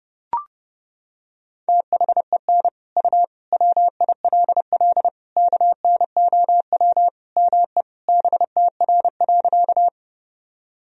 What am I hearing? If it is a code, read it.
Morse code, heard as a keyed tone, about 30 wpm, 700 Hz. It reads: THEN U WILL KNOW ME BTR.
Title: Sentences from Top 400 Words and Abbreviations 30wpm